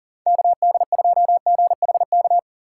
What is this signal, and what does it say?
Morse code, heard as a keyed tone, about 40 wpm, 700 Hz. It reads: KD2GHK
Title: USA Call Signs - 3x Character Spacing - 40wpm